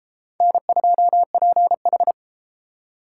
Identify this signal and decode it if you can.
Morse code, heard as a keyed tone, about 31 wpm, 700 Hz. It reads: N2PH